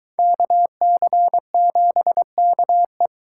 Morse code, heard as a keyed tone, about 23 wpm, 700 Hz. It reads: KC7KE